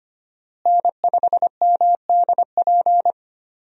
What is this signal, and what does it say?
Morse code, heard as a keyed tone, about 25 wpm, 700 Hz. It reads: N5MDP
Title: USA Call Signs - 5 Characters - Repeated 3x - 25wpm